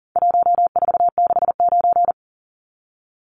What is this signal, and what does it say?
Morse code, heard as a keyed tone, about 40 wpm, 700 Hz. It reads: 1469
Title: Numbers - 4-Digit 40wpm